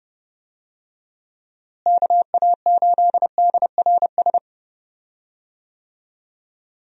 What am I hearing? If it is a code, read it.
Morse code, heard as a keyed tone, about 31 wpm, 700 Hz. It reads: KA8DRS
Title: USA Call Signs - 6 Characters - Repeated 2x - 30wpm